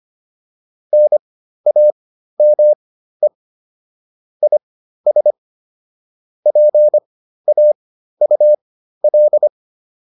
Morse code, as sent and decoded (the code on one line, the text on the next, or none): -. .- -- . / .. ... / .--. .- ..- .-..
NAME IS PAUL